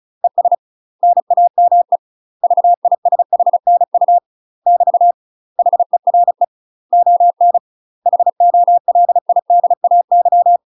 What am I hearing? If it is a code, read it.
Morse code, heard as a keyed tone, about 34 wpm, 700 Hz. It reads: ES NAME VISHNU <BT> HERE ON HOLIDAY
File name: UK MASTER QSOs TRAINING/35WPM